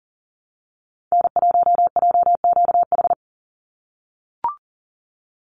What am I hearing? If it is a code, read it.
Morse code, heard as a keyed tone, about 40 wpm, 700 Hz. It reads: N1JQH
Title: USA Call Signs 40wpm